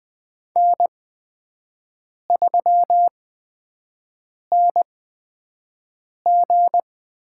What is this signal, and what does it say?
Morse code, heard as a keyed tone, about 20 wpm, spaced as for 5 wpm, 700 Hz. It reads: N3NG
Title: USA Call Signs - 8x Character Spacing - 20wpm